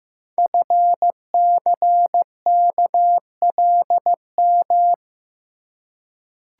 Morse code, heard as a keyed tone, about 15 wpm, 700 Hz. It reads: FCKLM
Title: Code Groups - 5 Letters - 15wpm